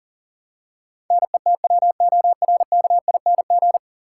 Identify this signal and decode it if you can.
Morse code, heard as a keyed tone, about 40 wpm, 700 Hz. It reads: NETWORKING
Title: Practice QSO - Exchange 3 - More Info - 40wpm